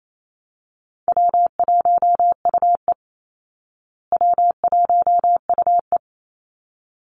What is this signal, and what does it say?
Morse code, heard as a keyed tone, about 28 wpm, 700 Hz. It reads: W1UE W1UE
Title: Call Signs Repeated 3x - 28wpm